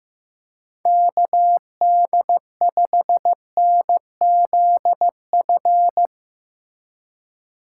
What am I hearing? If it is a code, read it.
Morse code, heard as a keyed tone, about 15 wpm, 700 Hz. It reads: KD5NZF
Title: USA Call Signs - 6 Characters - 2x Character Spacing - 15wpm